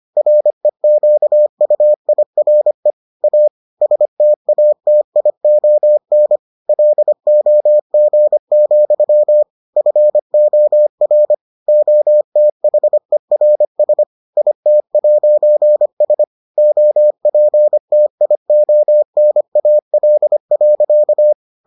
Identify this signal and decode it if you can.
Morse code, heard as a keyed tone, about 25 wpm, 600 Hz. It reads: REQUIRE A STATION LOG, FOR OTHERS IT'S OPTIONAL.